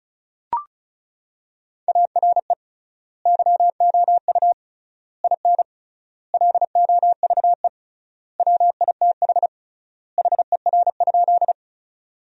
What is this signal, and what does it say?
Morse code, heard as a keyed tone, about 34 wpm, 700 Hz. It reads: ARE YOU IN LOVE WITH HER?
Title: 3x Word-Spacing - Sentences from Top 400 Words - 35wpm